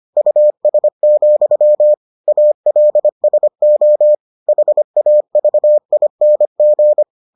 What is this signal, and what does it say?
Morse code, heard as a keyed tone, about 25 wpm, 600 Hz. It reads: US, ALSO HAVING